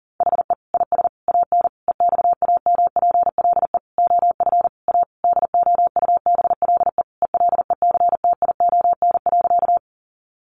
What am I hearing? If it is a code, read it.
Morse code, heard as a keyed tone, about 40 wpm, 700 Hz. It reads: HE IS AN EXAMPLE OF A DOUBLE ELECTION.
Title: Sentences from Top 1500 Words 40wpm